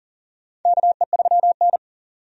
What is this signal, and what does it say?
Morse code, heard as a keyed tone, about 37 wpm, 700 Hz. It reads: KE3N
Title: USA Call Signs - 4 Characters - 40wpm